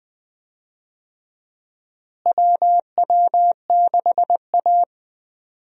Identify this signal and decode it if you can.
Morse code, heard as a keyed tone, about 21 wpm, 700 Hz. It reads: WW6A
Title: USA Call Signs - 4 Characters - Repeated 2x - 20wpm